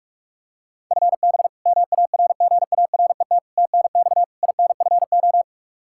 Morse code, heard as a keyed tone, about 42 wpm, 700 Hz. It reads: FB MARGARET TNX INFO
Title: Practice QSO - Exchange 3 - More Info - 45wpm